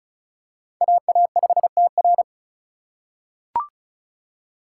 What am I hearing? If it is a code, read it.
Morse code, heard as a keyed tone, about 36 wpm, 700 Hz. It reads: AA5TR
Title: USA Call Signs - 5 Characters - 3x Character Spacing - 35wpm